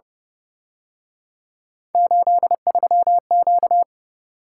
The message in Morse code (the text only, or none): 83Q